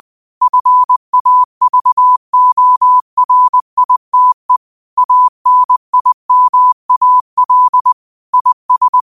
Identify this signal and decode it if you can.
Morse code, heard as a keyed tone, about 20 wpm, 1000 Hz. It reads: FAVORITE ANIMAL IS
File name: 20wpm